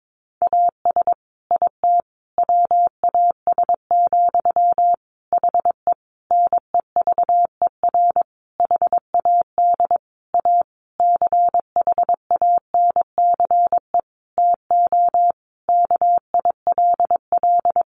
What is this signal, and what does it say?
Morse code, heard as a keyed tone, about 22 wpm, 700 Hz. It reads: AS IT WAS, HE NEVER HAD A CHANCE TO KILL
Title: Sentences from Top 1400 Words 22wpm